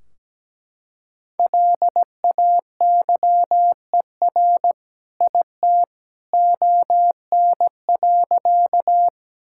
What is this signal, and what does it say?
Morse code, heard as a keyed tone, about 17 wpm, 700 Hz. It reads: LAYER IT ON.
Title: Sentences from Top 1600 Words 17wpm